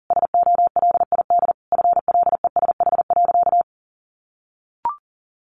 Morse code, heard as a keyed tone, about 40 wpm, 700 Hz. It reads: SOLID FLESH.
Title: A Sound Of Thunder 40wpm